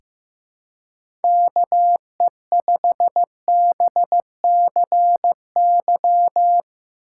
Morse code, heard as a keyed tone, about 15 wpm, 700 Hz. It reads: KE5BCY
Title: K1USN SST QSO Practice - 15wpm